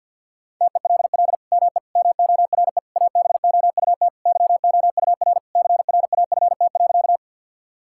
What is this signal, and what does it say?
Morse code, heard as a keyed tone, about 52 wpm, 700 Hz. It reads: TELL ME MORE ABOUT YOUR CRAFT.
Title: Sentences from Top 1000 Words - Review Part 6 - 50wpm